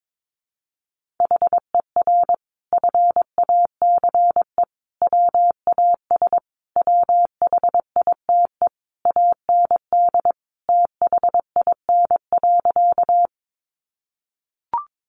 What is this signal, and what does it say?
Morse code, heard as a keyed tone, about 22 wpm, 700 Hz. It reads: HER FACE WAS WHITE AND THIN.
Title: Sentences from Top 1200 Words 22wpm